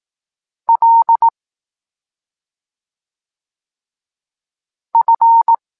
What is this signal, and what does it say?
Morse code, heard as a keyed tone, about 18 wpm, 920 Hz. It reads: L F
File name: Tekst-26 1